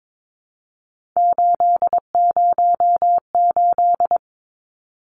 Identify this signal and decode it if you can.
Morse code, heard as a keyed tone, about 22 wpm, 700 Hz. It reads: 808
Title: Numbers - 3-Digit 22wpm